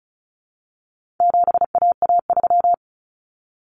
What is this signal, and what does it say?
Morse code, heard as a keyed tone, about 34 wpm, 700 Hz. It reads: 7AA3
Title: Common Cut Numbers - 4-Digit 35wpm